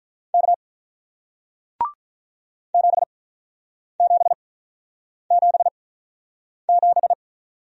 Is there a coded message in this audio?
Yes: Morse code from 0.3 to 7.1 s, heard as a keyed tone, about 47 wpm, spaced as for 9 wpm, 700 Hz. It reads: K 7777